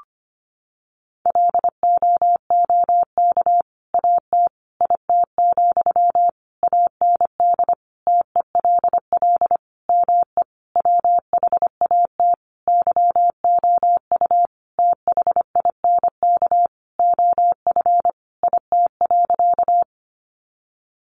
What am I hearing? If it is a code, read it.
Morse code, heard as a keyed tone, about 25 wpm, 700 Hz. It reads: LOOK AT IT, AND TELL ME WHAT YOU THINK OF IT.
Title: Sentences from Top 200 Words 25wpm